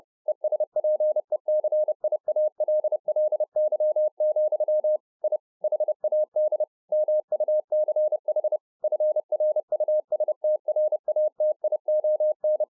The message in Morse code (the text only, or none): ESPECIALLY, I HAD MUCH FRUSTRATION